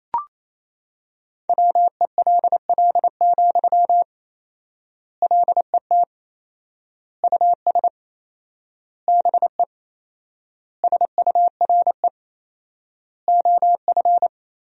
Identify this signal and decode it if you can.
Morse code, heard as a keyed tone, about 28 wpm, 700 Hz. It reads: WELL, LET US BE SURE OF
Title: 4x Word-Spacing - Sentences from Top 300 Words - 28wpm